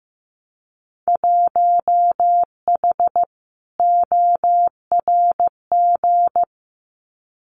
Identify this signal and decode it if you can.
Morse code, heard as a keyed tone, about 15 wpm, 700 Hz. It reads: 1H ORG
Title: Winter Field Day Exchanges 15wpm